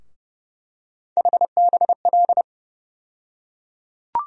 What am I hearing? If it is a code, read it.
Morse code, heard as a keyed tone, about 32 wpm, 700 Hz. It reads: HBL
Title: USA Call Sign Postfixes - 3 Letters 30wpm